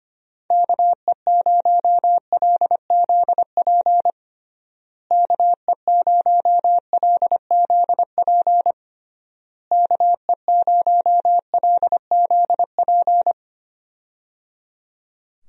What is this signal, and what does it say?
Morse code, heard as a keyed tone, about 25 wpm, 700 Hz. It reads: KE0LZP KE0LZP KE0LZP
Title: USA Call Signs - 6 Characters - Repeated 3x - 25wpm